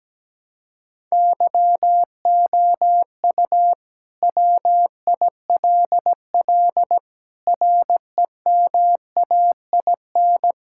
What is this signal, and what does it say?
Morse code, heard as a keyed tone, about 17 wpm, 700 Hz. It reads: YOU WILL REMAIN
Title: Sentences from Top 1600 Words 17wpm